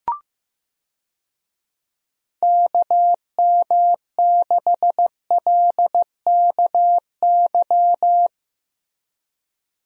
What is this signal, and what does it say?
Morse code, heard as a keyed tone, about 15 wpm, 700 Hz. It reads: KM6LKY
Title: USA Call Signs - 6 Characters - Repeated 2x - 15wpm